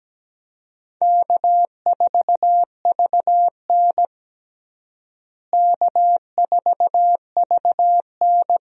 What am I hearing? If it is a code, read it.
Morse code, heard as a keyed tone, about 17 wpm, 700 Hz. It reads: K4VN K4VN
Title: USA Call Signs - 4 Characters - Repeated 2x - 17wpm